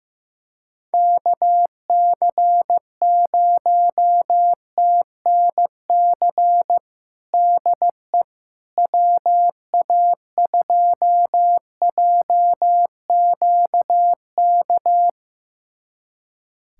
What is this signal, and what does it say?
Morse code, heard as a keyed tone, about 15 wpm, 700 Hz. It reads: KC0TNC DE WA2JQK
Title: USA Callsign DE Callsign - 6 Characters - 15wpm